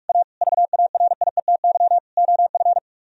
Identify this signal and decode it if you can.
Morse code, heard as a keyed tone, about 49 wpm, 700 Hz. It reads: A VARIETY OF